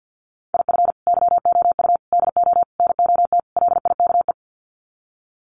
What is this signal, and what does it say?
Morse code, heard as a keyed tone, about 50 wpm, 700 Hz. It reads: IF YOU DO NOT LIKE
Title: Sets of 5 Words 50wpm